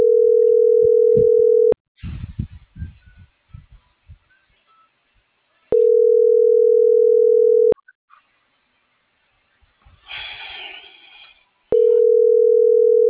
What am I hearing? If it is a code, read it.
Ringback tone (North America), 440 + 480 Hz.